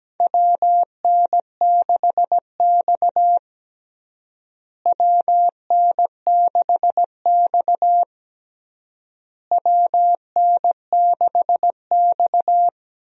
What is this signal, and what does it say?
Morse code, heard as a keyed tone, about 17 wpm, 700 Hz. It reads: WN6X WN6X WN6X
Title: USA Call Signs - 4 Characters - Repeated 3x - 17wpm